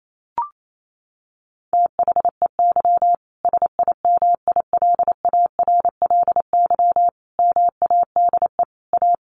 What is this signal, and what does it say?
Morse code, heard as a keyed tone, about 28 wpm, 700 Hz. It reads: THEY SIMILARLY MADE A
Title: Sentences from Top 1400 Words 28wpm